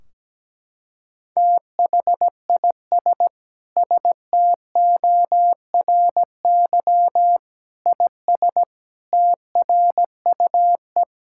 Morse code, heard as a keyed tone, about 17 wpm, 700 Hz. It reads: THIS STORY IS TRUE
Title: Sentences from Top 1600 Words 17wpm